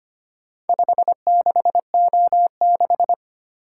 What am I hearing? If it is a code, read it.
Morse code, heard as a keyed tone, about 25 wpm, 700 Hz. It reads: 56O6
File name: practice / 25wpm / Four Letter-Number Combinations - 25wpm